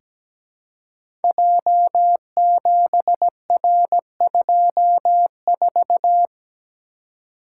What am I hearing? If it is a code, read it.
Morse code, heard as a keyed tone, about 17 wpm, 700 Hz. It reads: J7R24
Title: Code Groups - 5 Letter:Numbers - 17wpm